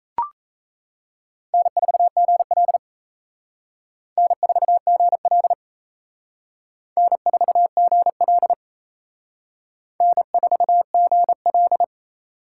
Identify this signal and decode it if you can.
Morse code, heard as a keyed tone, about 34 wpm, 700 Hz. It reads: N4GL N4GL N4GL N4GL
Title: Speed Racing - USA 1x2 Call Signs 28wpm